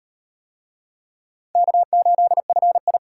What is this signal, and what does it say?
Morse code, heard as a keyed tone, about 38 wpm, 700 Hz. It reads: K8FI